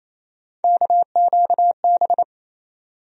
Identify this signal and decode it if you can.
Morse code, heard as a keyed tone, about 28 wpm, 700 Hz. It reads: KQB
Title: USA Call Sign Postfixes - 3 Letters 28wpm